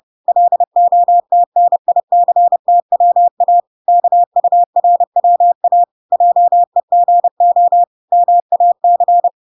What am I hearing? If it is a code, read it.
Morse code, heard as a keyed tone, about 30 wpm, 700 Hz. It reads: LOTNICTWA KURWA JEGO MAC